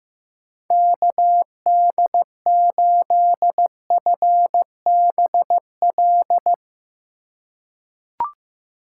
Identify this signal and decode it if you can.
Morse code, heard as a keyed tone, about 15 wpm, 700 Hz. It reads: KD8FBL
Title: USA Call Signs - 6 Characters - 2x Character Spacing - 15wpm